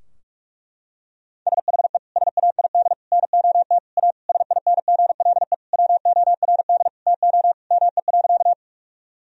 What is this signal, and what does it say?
Morse code, heard as a keyed tone, about 46 wpm, 700 Hz. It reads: SHE SAID NOT A SINGLE WORD TO ME.
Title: Sentences from Top 1000 Words - Review Part 4 - 45wpm